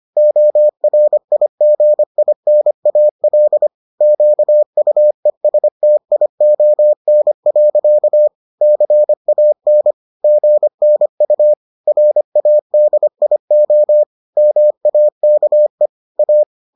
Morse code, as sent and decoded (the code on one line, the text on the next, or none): --- .-. .. --. .. -. .- .-.. / --.- ..- . ... - .. --- -. .-.-.- / -.-. .- -. / --. -. ..- / .-. .- -.. .. --- / -- .- -.- . / .-
ORIGINAL QUESTION. CAN GNU RADIO MAKE A